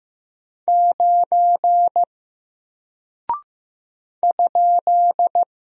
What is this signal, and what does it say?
Morse code, heard as a keyed tone, about 15 wpm, 700 Hz. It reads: 9 ?
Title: Single Character (Letter:Number:Punctuation:Prosign) Alt 15wpm